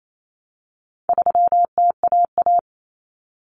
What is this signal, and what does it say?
Morse code, heard as a keyed tone, about 28 wpm, 700 Hz. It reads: 3TAA
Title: Common Cut Numbers - 4-Digit 28wpm